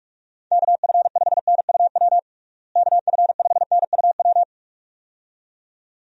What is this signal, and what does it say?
Morse code, heard as a keyed tone, about 42 wpm, 700 Hz. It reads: KF5NUW KF5NUW
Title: USA Call Signs - 6 Characters - Repeated 2x - 45wpm